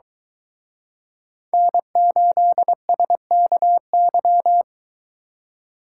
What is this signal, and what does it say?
Morse code, heard as a keyed tone, about 23 wpm, 700 Hz. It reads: N8SKY